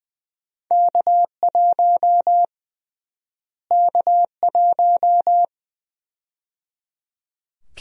Morse code, heard as a keyed tone, about 20 wpm, 700 Hz. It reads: K1 K1
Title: USA Call Sign Prefixes - Repeated 2x - 20wpm